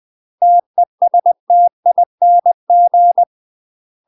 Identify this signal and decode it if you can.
Morse code, heard as a keyed tone, about 20 wpm, 700 Hz. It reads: TESTING
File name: UK MASTER QSOs TRAINING / 20WPM / UK MASTER QSOs 0070